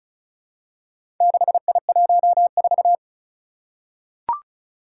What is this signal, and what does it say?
Morse code, heard as a keyed tone, about 36 wpm, 700 Hz. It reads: 6I14
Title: Four Challenging Letter-Number Combinations 35wpm